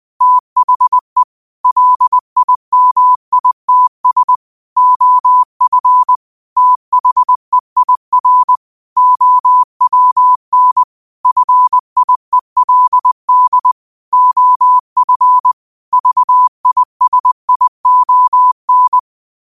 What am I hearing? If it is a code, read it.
Morse code, heard as a keyed tone, about 20 wpm, 1000 Hz. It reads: THE LIMITS OF THEIR OWN FIELD OF VISION
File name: qotdfiles / 20wpm